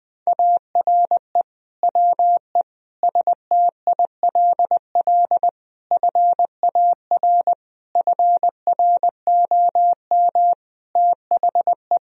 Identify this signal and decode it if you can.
Morse code, heard as a keyed tone, about 20 wpm, 700 Hz. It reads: ARE WE STILL FAR FROM THE
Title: Sentences from Top 1000 Words - Review Part 4 - 20wpm